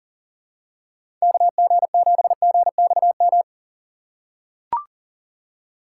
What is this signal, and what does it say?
Morse code, heard as a keyed tone, about 38 wpm, 700 Hz. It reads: KG7GXM
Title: USA Call Signs - 6 Characters - 4x Character Spacing - 40wpm